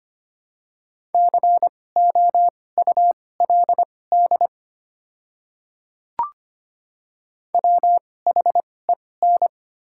Morse code, heard as a keyed tone, about 25 wpm, spaced as for 18 wpm, 700 Hz. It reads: COULD WHEN
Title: 2x Character Spacing - Top 200 Words - 25wpm